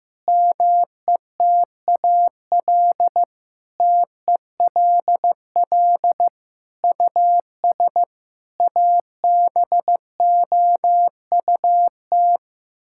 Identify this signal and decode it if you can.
Morse code, heard as a keyed tone, about 14 wpm, 700 Hz. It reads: METAL TELL US ABOUT
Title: Sentences from Top 1000 Words - Review Part 6 - 15wpm